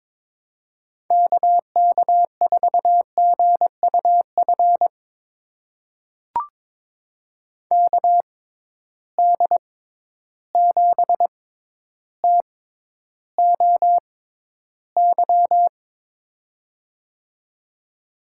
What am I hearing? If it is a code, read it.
Morse code, heard as a keyed tone, about 22 wpm, spaced as for 11 wpm, 700 Hz. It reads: KK4GUF KD7TOY
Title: USA Call Signs - 6 Characters - 6x Character Spacing - 22wpm